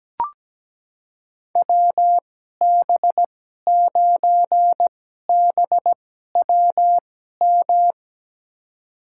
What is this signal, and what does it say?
Morse code, heard as a keyed tone, about 17 wpm, spaced as for 12 wpm, 700 Hz. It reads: WB9BWM
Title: USA Call Signs - 6 Characters - 2x Character Spacing - 17wpm